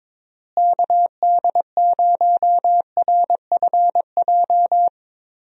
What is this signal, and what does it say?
Morse code, heard as a keyed tone, about 22 wpm, 700 Hz. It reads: KD0RFJ
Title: USA Call Signs - 6 Characters - Repeated 3x - 22wpm